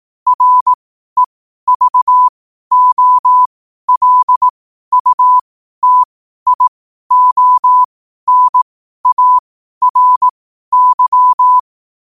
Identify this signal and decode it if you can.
Morse code, heard as a keyed tone, about 18 wpm, spaced as for 12 wpm, 1000 Hz. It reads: REVOLUTIONARY
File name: qotdfiles / 12wpm